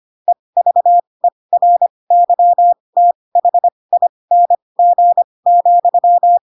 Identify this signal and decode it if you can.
Morse code, heard as a keyed tone, about 25 wpm, spaced as for 20 wpm, 700 Hz. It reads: EVERYTHING,
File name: Pinocchio / Pinocchio ch7 15 25 ik1hge